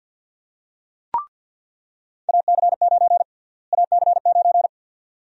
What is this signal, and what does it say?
Morse code, heard as a keyed tone, about 47 wpm, 700 Hz. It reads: AC9 AC9
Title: USA Call Sign Prefixes - Repeated 2x - 50wpm